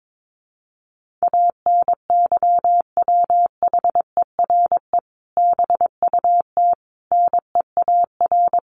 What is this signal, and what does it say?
Morse code, heard as a keyed tone, about 22 wpm, 700 Hz. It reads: ANYWHERE BUT NEAR
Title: Sentences from Top 1400 Words 22wpm